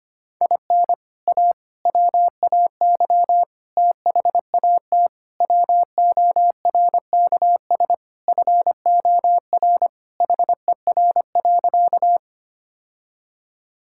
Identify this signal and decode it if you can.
Morse code, heard as a keyed tone, about 25 wpm, 700 Hz. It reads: IN A WAY THAT WORKS FOR HER.
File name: practice / 25wpm / Sentences with 4-Word Phrases - 25wpm